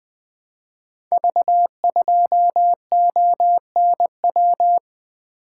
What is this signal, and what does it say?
Morse code, heard as a keyed tone, about 20 wpm, 700 Hz. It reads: V2ONW
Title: Code Groups - 5 Letter:Numbers - 20wpm